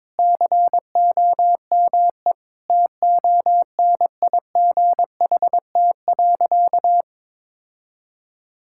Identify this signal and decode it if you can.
Morse code, heard as a keyed tone, about 22 wpm, 700 Hz. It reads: COME TONIGHT.
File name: practice / 22wpm / Sentences with 4-Word Phrases - 22wpm